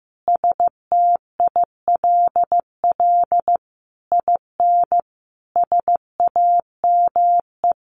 Morse code, heard as a keyed tone, about 15 wpm, 700 Hz. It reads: STILL IN SAME
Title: Sentences from Top 1000 Words 15wpm